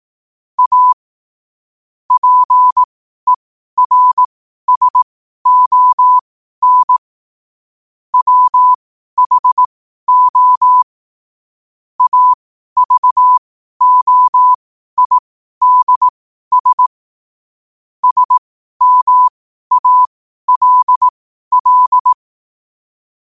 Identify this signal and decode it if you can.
Morse code, heard as a keyed tone, about 18 wpm, spaced as for 12 wpm, 1000 Hz. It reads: A PERSON WHO AVOIDS SMALL
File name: qotdfiles / 12wpm